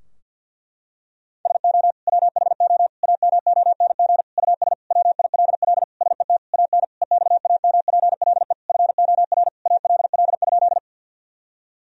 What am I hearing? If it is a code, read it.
Morse code, heard as a keyed tone, about 50 wpm, 700 Hz. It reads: SO WHO AMONG US WILL SET AN EXAMPLE FOR ALL?
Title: Sentences from Top 1000 Words - Review Part 3 - 50wpm